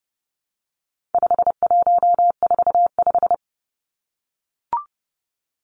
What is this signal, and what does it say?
Morse code, heard as a keyed tone, about 30 wpm, 700 Hz. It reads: 5145
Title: Numbers - 4-Digit 30wpm